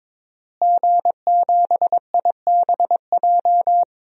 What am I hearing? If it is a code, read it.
Morse code, heard as a keyed tone, about 22 wpm, 700 Hz. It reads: G7IBJ
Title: DXCC-100 Call Signs - 5 Characters - Repeated 3x - 22wpm